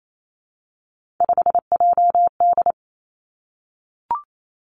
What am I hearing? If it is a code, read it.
Morse code, heard as a keyed tone, about 28 wpm, 700 Hz. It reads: 5JD
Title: Three Challenging Letter-Number Combinations 28wpm